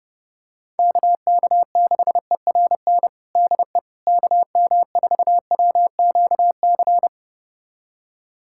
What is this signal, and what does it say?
Morse code, heard as a keyed tone, about 30 wpm, 700 Hz. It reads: KK6ERN DE KM4WQC
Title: USA Callsign DE Callsign - 6 Characters - 30wpm